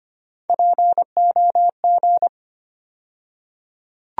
Morse code, heard as a keyed tone, about 25 wpm, 700 Hz. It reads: POG
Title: USA Call Sign Postfixes - 3 Letters 25wpm